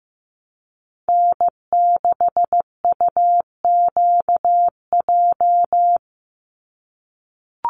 Morse code, heard as a keyed tone, about 15 wpm, 700 Hz. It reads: N6UQJ
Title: USA Call Signs 15wpm